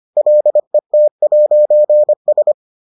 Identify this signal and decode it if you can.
Morse code, heard as a keyed tone, about 25 wpm, 600 Hz. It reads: LET'S